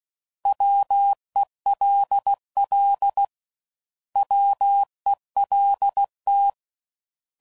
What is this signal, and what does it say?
Morse code, heard as a keyed tone, about 16 wpm, 800 Hz. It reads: WELL WELT
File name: Prefixes / By First Letter / Ws